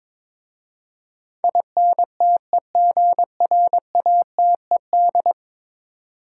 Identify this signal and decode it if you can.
Morse code, heard as a keyed tone, about 21 wpm, 700 Hz. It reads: INTEGRATED
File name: practice / 22wpm / 10 Letter Words - First 3 Letters Given - 22wpm